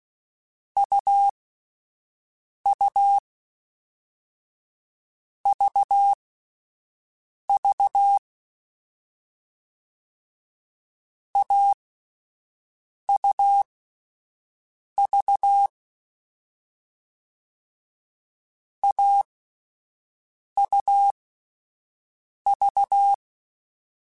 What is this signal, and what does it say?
Morse code, heard as a keyed tone, about 16 wpm, spaced as for 5 wpm, 800 Hz. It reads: UU VV AUV AUV